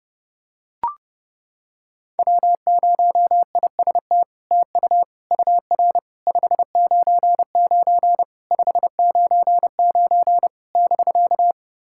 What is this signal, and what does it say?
Morse code, heard as a keyed tone, about 32 wpm, 700 Hz. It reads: W0IST TU UR 599 599 <BK>
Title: SOTA QSO Practice 30wpm